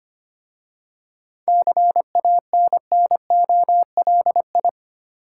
Morse code, heard as a keyed tone, about 25 wpm, 700 Hz. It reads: CANNOLI
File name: practice / 25wpm / Difficult Words with Hints - 25wpm